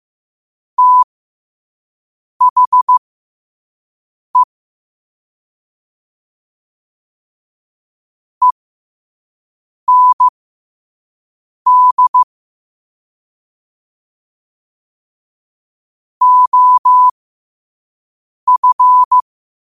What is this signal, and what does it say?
Morse code, heard as a keyed tone, about 15 wpm, spaced as for 5 wpm, 1000 Hz. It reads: THE END OF
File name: qotdfiles / 5wpm